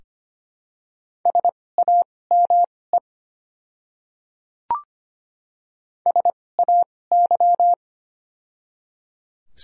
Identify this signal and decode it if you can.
Morse code, heard as a keyed tone, about 25 wpm, spaced as for 18 wpm, 700 Hz. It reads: SAME SAY
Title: 2x Character Spacing - Top 200 Words - 25wpm